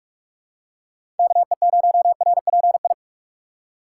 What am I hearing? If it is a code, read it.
Morse code, heard as a keyed tone, about 46 wpm, 700 Hz. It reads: KE0RPI
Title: USA Call Signs - 6 Characters - 4x Character Spacing - 45wpm